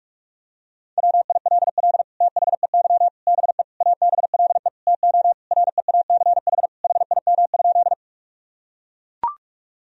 Morse code, heard as a keyed tone, about 41 wpm, 700 Hz. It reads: WILL THEY BE ABLE TO REACH HIM?
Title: Sentences from Top 1000 Words - Review Part 2 - 45wpm